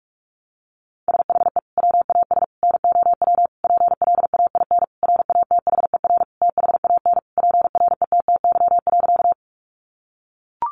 Morse code, heard as a keyed tone, about 44 wpm, 700 Hz. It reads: SHE WAS NOW PLAIN RATHER THAN PRETTY.
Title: Sentences from Top 1000 Words 45wpm